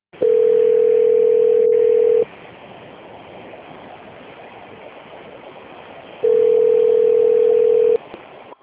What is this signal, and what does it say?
Ringback tone (North America), 440 + 480 Hz.